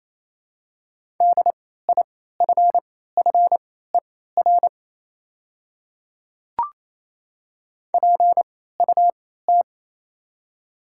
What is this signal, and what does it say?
Morse code, heard as a keyed tone, about 27 wpm, spaced as for 16 wpm, 690 Hz. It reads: DIFFER PUT
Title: 3x Character Spacing - Top 200 Words - 28wpm